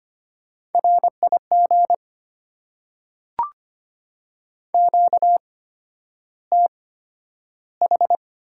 Morse code, heard as a keyed tone, about 25 wpm, spaced as for 11 wpm, 700 Hz. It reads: RIG QTH
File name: Top 90 QSO Elements - 8x Character Spacing - 25wpm